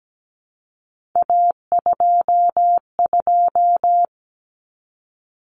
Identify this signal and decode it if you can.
Morse code, heard as a keyed tone, about 17 wpm, 700 Hz. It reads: A22
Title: Common Cut Numbers - 3-Digit 17wpm